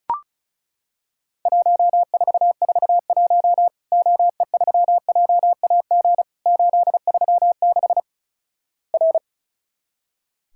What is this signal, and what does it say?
Morse code, heard as a keyed tone, about 36 wpm, 700 Hz. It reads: 1441 OE3JAG 836 R
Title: Worked All Europe DX Contest - QTC Exchange - 35wpm